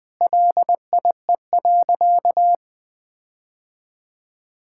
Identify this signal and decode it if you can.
Morse code, heard as a keyed tone, about 19 wpm, 700 Hz. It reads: LIE.
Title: Sentences from Top 1600 Words 20wpm